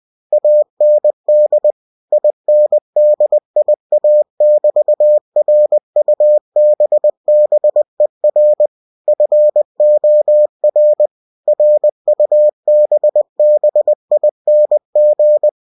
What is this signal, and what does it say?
Morse code, heard as a keyed tone, about 20 wpm, 600 Hz. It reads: AND INDIA<BT>RUBBER FOR RUBBING
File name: cw-books / winnie-the-pooh / winnie-20wpm-page-0109